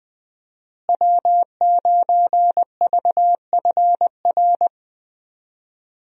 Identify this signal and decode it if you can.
Morse code, heard as a keyed tone, about 20 wpm, 700 Hz. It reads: W9VFR
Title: USA Call Signs - 5 Characters - Repeated 3x - 20wpm